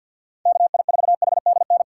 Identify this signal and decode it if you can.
Morse code, heard as a keyed tone, about 51 wpm, 700 Hz. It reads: KI4HDN
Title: USA Call Signs - 6 Characters - 6x Character Spacing - 50wpm